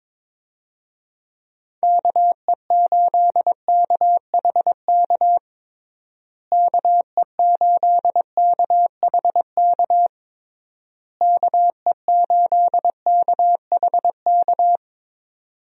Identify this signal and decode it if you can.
Morse code, heard as a keyed tone, about 22 wpm, 700 Hz. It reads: KE8KHK KE8KHK KE8KHK
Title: USA Call Signs - 6 Characters - Repeated 3x - 22wpm